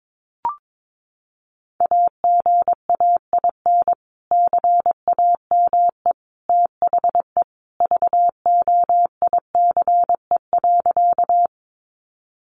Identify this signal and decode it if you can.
Morse code, heard as a keyed tone, about 22 wpm, 700 Hz. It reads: AGAIN CAME THE VOICE.
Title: Sentences from Top 500 Words 22wpm